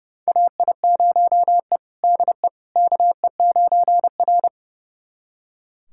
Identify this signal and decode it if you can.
Morse code, heard as a keyed tone, about 30 wpm, 700 Hz. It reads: AI0E DE KE9R
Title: USA Callsign DE Callsign - 4 Characters - 30wpm